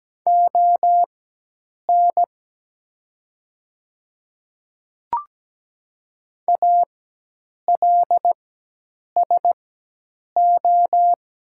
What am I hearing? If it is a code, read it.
Morse code, heard as a keyed tone, about 17 wpm, spaced as for 8 wpm, 700 Hz. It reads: ON ALSO
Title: 4x Character Spacing - Top 200 Words - 17wpm